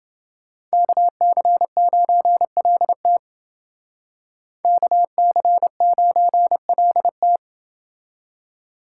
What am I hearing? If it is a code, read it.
Morse code, heard as a keyed tone, about 29 wpm, 700 Hz. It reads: KC9LT KC9LT